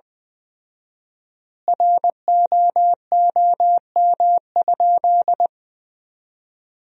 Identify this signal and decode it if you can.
Morse code, heard as a keyed tone, about 20 wpm, 700 Hz. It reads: ROOM?
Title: 4x Word-Spacing - Sentences from Top 300 Words - 20wpm